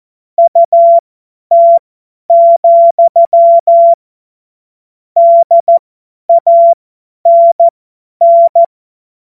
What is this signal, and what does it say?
Morse code, heard as a keyed tone, about 14 wpm, spaced as for 10 wpm, 680 Hz. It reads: UT, DANN
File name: Morsekurs / Witze / Witz 21.November 2023 50 F70